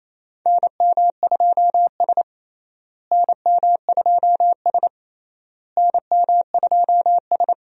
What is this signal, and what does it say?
Morse code, heard as a keyed tone, about 28 wpm, 700 Hz. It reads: NM2S NM2S NM2S
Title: USA Call Signs - 4 Characters - Repeated 3x - 28wpm